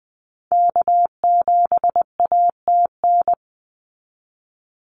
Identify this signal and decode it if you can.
Morse code, heard as a keyed tone, about 20 wpm, 700 Hz. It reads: K7ATN
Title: NA SOTA Activator Callsigns 20wpm